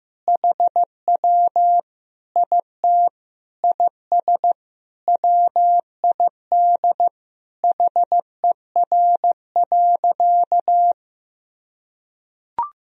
Morse code, heard as a keyed tone, about 15 wpm, 700 Hz. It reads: HW IT IS WID HER.
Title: Sentences from Top 100 Words and Abbreviations 15wpm